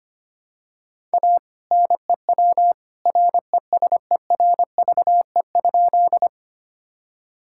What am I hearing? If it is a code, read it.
Morse code, heard as a keyed tone, about 25 wpm, 700 Hz. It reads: A NEW RESERVE?
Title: Sentences from Top 1000 Words - Review Part 6 - 25wpm